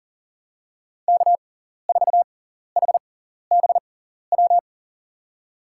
Morse code, heard as a keyed tone, about 38 wpm, spaced as for 14 wpm, 700 Hz. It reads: K4HBW
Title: USA Call Signs - 5 Characters - 6x Character Spacing - 40wpm